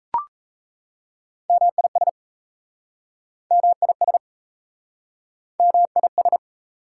Morse code, heard as a keyed tone, about 38 wpm, 700 Hz. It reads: MIS MIS MIS